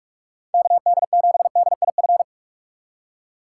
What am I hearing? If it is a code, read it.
Morse code, heard as a keyed tone, about 41 wpm, 700 Hz. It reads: KD7DIF